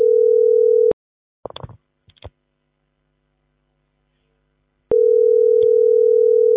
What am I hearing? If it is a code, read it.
Ringback tone (North America), 440 + 480 Hz.